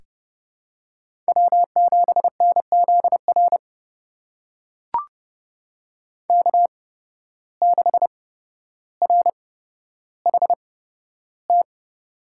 Morse code, heard as a keyed tone, about 30 wpm, spaced as for 13 wpm, 700 Hz. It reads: W7NZR K6RHT
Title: USA Call Signs - 8x Character Spacing - 30wpm